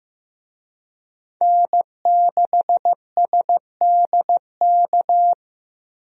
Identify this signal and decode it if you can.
Morse code, heard as a keyed tone, about 15 wpm, 700 Hz. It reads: N6SDK
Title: USA Call Signs - 5 Characters - 15wpm